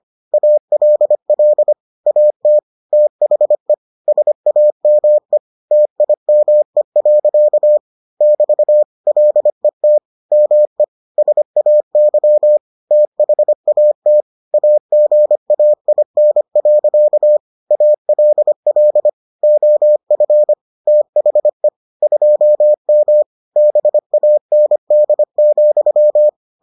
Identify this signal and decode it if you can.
Morse code, heard as a keyed tone, about 25 wpm, 600 Hz. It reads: ALL AT THE SAME TIME. <BT> LET ME SAY THAT AGAIN. ALL OF THE 2M BAND,